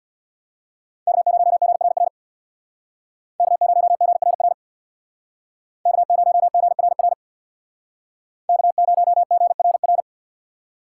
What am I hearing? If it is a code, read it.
Morse code, heard as a keyed tone, about 60 wpm, 700 Hz. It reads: K0GRR K0GRR K0GRR K0GRR
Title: Speed Racing - USA 1x3 Call Signs 50wpm